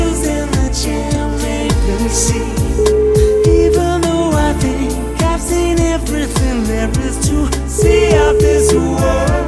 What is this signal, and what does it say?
Ringback tone (Europe, 425 Hz), with noise.